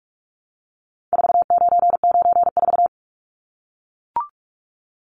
Morse code, heard as a keyed tone, about 44 wpm, 700 Hz. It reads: 4994
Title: Numbers - 4-Digit 45wpm